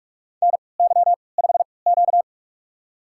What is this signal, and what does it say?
Morse code, heard as a keyed tone, about 46 wpm, spaced as for 26 wpm, 700 Hz. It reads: NY5Q
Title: USA Call Signs - 4 Characters - 3x Character Spacing - 45wpm